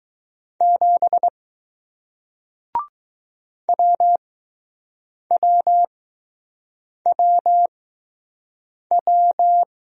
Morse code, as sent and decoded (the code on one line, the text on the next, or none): --... / .-- .-- .-- .--
7 WWWW